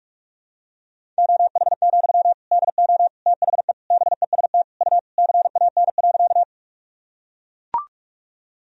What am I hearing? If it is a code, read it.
Morse code, heard as a keyed tone, about 41 wpm, 700 Hz. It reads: OH, DO THE BEST U CAN.
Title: Sentences from Top 500 Words and Abbreviations 45wpm